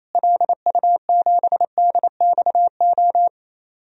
Morse code, heard as a keyed tone, about 28 wpm, 700 Hz. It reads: LU7DXO
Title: DXCC-100 Call Signs - 6 Characters - Repeated 2x - 28wpm